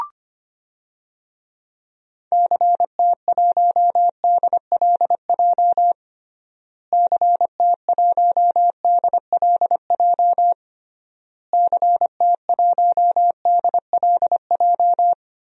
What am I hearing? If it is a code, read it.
Morse code, heard as a keyed tone, about 25 wpm, 700 Hz. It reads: CT1DLJ CT1DLJ CT1DLJ
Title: DXCC-100 Call Signs - 6 Characters - Repeated 3x - 25wpm